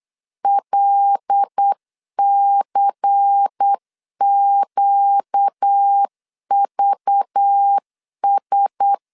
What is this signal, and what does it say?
Morse code, heard as a keyed tone, about 8 wpm, 800 Hz. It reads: LCQVS
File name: LEZIONI / 7 GF9Z / 7 GF 8wpm 3108